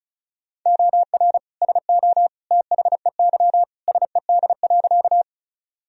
Morse code, heard as a keyed tone, about 36 wpm, 700 Hz. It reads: OR SO THEY SED.
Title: Sentences from Top 100 Words and Abbreviations 35wpm